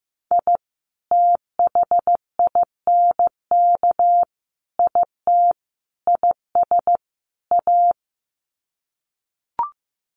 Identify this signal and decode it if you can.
Morse code, heard as a keyed tone, about 15 wpm, 700 Hz. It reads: I THINK IT IS A
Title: Sets of 5 Words Encore 15wpm